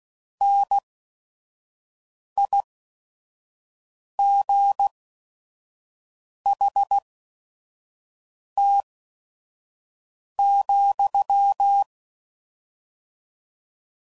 Morse code, heard as a keyed tone, about 16 wpm, spaced as for 5 wpm, 800 Hz. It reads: NIGHT,
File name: CW Code Groups @5WPM / Weather Sets / Weather 4 17